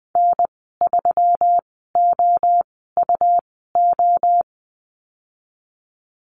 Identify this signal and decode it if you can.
Morse code, heard as a keyed tone, about 20 wpm, spaced as for 15 wpm, 700 Hz. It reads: N3OUO
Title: 186 Lesson 30g - Letter b - Call Signs 20wpm